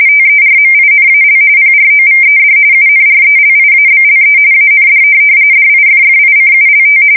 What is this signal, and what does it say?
RTTY, about 45.45 baud. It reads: TERMINAL PROVED UNRELIABLE AND DIFFICULT
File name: fsk170 / teletype-corp